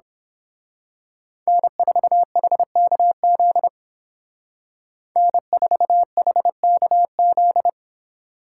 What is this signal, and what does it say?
Morse code, heard as a keyed tone, about 30 wpm, 700 Hz. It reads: N4HKZ N4HKZ